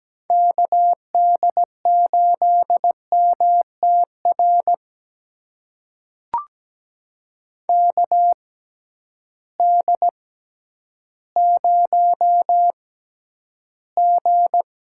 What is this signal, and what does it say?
Morse code, heard as a keyed tone, about 17 wpm, spaced as for 10 wpm, 700 Hz. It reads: KD8MTR KD0G
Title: USA Call Signs - 6 Characters - 6x Character Spacing - 17wpm